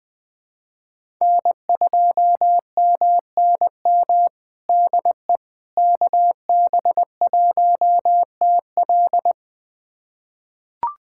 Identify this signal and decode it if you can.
Morse code, heard as a keyed tone, about 20 wpm, 700 Hz. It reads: N2MNM DE KB1TL
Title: USA Callsign DE Callsign - 5 Characters - 20wpm